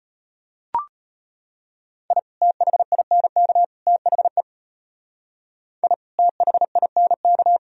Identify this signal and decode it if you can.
Morse code, heard as a keyed tone, about 30 wpm, 700 Hz. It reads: I THINK THE I THINK
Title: Speed-Racing - Sets of 3 Words 25wpm